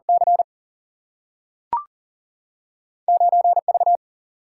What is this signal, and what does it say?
Morse code, heard as a keyed tone, about 39 wpm, 700 Hz. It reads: C 9 V
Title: Single-Letter Single-Number 40wpm